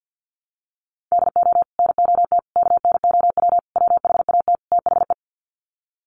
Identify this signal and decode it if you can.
Morse code, heard as a keyed tone, about 50 wpm, 700 Hz. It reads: DO NOT KNOW WHAT THE
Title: Sets of 5 Words 50wpm